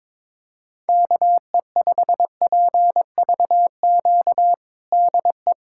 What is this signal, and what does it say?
Morse code, heard as a keyed tone, about 21 wpm, 700 Hz. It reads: KE5PVQ DE
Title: USA Callsign DE Callsign 22wpm